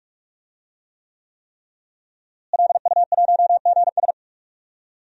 Morse code, heard as a keyed tone, about 46 wpm, 700 Hz. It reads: LU1GS
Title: DXCC-100 Call Signs - 5 Characters - Repeated 3x - 45wpm